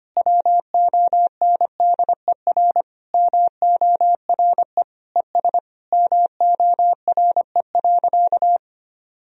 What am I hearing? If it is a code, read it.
Morse code, heard as a keyed tone, about 25 wpm, 700 Hz. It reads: WONDER MORE ES MORE.
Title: Sentences from Top 500 Words and Abbreviations 25wpm